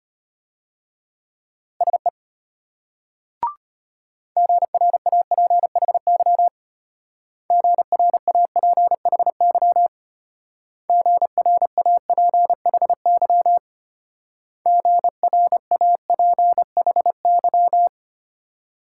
Morse code, heard as a keyed tone, about 32 wpm, 700 Hz. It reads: SE GRAPHY GRAPHY GRAPHY GRAPHY